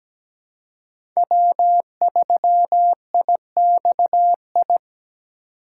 Morse code, heard as a keyed tone, about 17 wpm, 700 Hz. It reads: W3IXI
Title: USA Call Signs - 4x Character Spacing - 17wpm